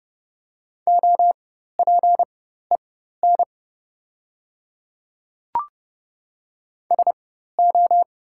Morse code, heard as a keyed tone, about 30 wpm, spaced as for 14 wpm, 700 Hz. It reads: OPEN SO
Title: 4x Character Spacing - Top 300 Words - 30wpm